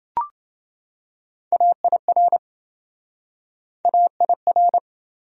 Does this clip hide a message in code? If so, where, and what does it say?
Yes: Morse code from 1.5 to 4.8 s, heard as a keyed tone, about 26 wpm, 700 Hz. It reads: AIR AIR